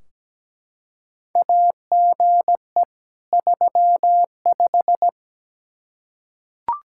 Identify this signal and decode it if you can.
Morse code, heard as a keyed tone, about 17 wpm, 700 Hz. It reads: AGE 35
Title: Practice QSO - Exchange 3 - More Info - 17wpm